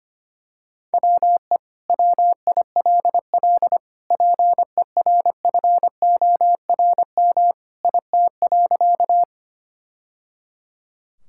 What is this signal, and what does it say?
Morse code, heard as a keyed tone, about 25 wpm, 700 Hz. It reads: WE WILL PERFORM IT.
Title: Sentences from Top 1000 Words - Review Part 5 - 25wpm